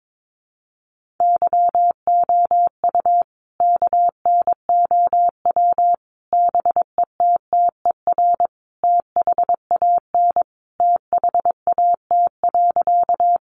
Morse code, heard as a keyed tone, about 22 wpm, 700 Hz. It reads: YOU KNOW BETTER THAN THAT.
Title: Sentences from Top 400 Words 22wpm